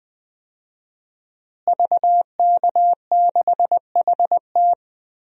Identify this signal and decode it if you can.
Morse code, heard as a keyed tone, about 21 wpm, 700 Hz. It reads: VK6HT
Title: DXCC-100 Call Signs - 5 Characters - Repeated 3x - 20wpm